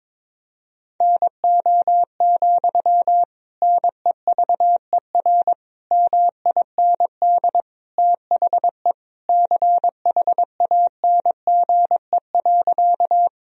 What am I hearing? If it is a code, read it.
Morse code, heard as a keyed tone, about 22 wpm, 700 Hz. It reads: NO, NEVER MIND THE CHANGE.
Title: Sentences from Top 1000 Words - Review Part 1 - 22wpm